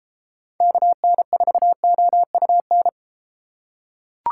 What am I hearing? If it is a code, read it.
Morse code, heard as a keyed tone, about 33 wpm, 700 Hz. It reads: KN4OUN